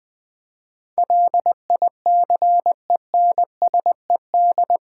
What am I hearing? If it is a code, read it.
Morse code, heard as a keyed tone, about 20 wpm, 700 Hz. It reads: LICENSED
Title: Practice QSO - Exchange 3 - More Info - 20wpm